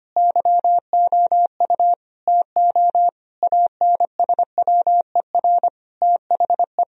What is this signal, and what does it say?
Morse code, heard as a keyed tone, about 25 wpm, 700 Hz. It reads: YOU TO ANSWER THE
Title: Sentences from Top 1000 Words - Review Part 1 - 25wpm